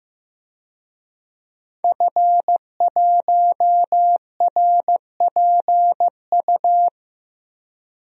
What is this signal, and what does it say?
Morse code, heard as a keyed tone, about 15 wpm, 700 Hz. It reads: F1RPU
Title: DXCC-100 Call Signs - 5 Characters - Repeated 2x - 15wpm